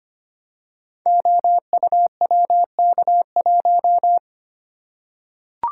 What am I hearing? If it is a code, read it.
Morse code, heard as a keyed tone, about 25 wpm, 700 Hz. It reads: OUWK1
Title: Code Groups - 5 Letter:Numbers - 25wpm